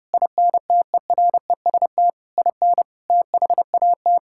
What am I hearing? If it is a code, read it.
Morse code, heard as a keyed tone, about 32 wpm, 700 Hz. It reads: INTEREST IN THAT
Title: Sentences from Top 1000 Words - Review Part 5 - 30wpm